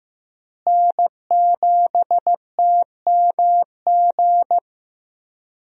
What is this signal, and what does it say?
Morse code, heard as a keyed tone, about 15 wpm, 700 Hz. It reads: N7TMG
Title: USA Call Signs - 5 Characters - Repeated 3x - 15wpm